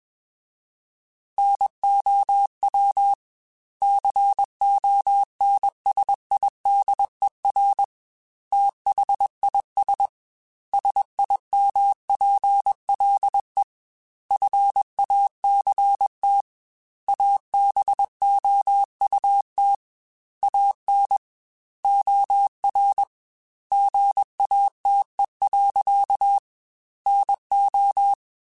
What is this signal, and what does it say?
Morse code, heard as a keyed tone, about 21 wpm, 800 Hz. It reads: NOW CONSIDER THIS SIMPLE FACT ABOUT AN OR GATE. NO
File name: Technical Subjects / Technical Subjects-mp3s / OR Gate @20